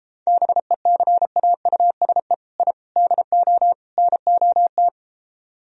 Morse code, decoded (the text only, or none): BECAUSE I DO NOT